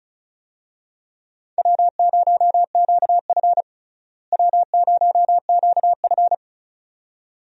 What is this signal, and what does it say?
Morse code, heard as a keyed tone, about 34 wpm, 700 Hz. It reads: W0QF W0QF
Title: USA Call Signs - 4 Characters - Repeated 2x - 35wpm